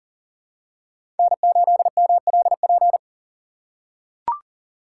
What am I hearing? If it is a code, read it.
Morse code, heard as a keyed tone, about 40 wpm, 700 Hz. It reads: N8MLP
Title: USA Call Signs - 5 Characters - 3x Character Spacing - 40wpm